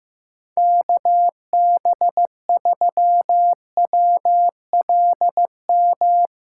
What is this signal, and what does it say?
Morse code, heard as a keyed tone, about 16 wpm, 700 Hz. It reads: KB3WLM
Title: Speed Racing - USA 2x3 Call Signs 15wpm